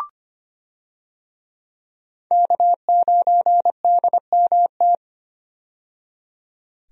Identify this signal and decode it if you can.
Morse code, heard as a keyed tone, about 25 wpm, 700 Hz. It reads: K9DMT
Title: USA Call Signs - 5 Characters - 25wpm